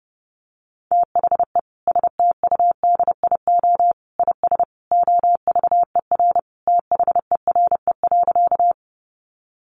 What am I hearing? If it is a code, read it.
Morse code, heard as a keyed tone, about 30 wpm, 700 Hz. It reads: THE STUDIO IS OVER THERE.
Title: Sentences from Top 1200 Words 30wpm